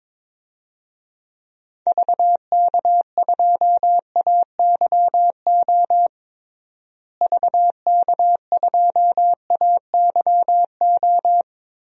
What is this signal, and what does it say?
Morse code, heard as a keyed tone, about 22 wpm, 700 Hz. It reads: VK2AYO VK2AYO
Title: DXCC-100 Call Signs - 6 Characters - Repeated 3x - 22wpm